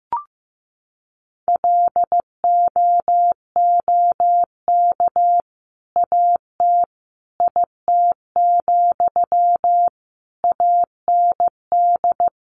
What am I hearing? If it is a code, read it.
Morse code, heard as a keyed tone, about 15 wpm, 700 Hz. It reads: LOOK AT IT, AND
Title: Sentences from Top 200 Words 15wpm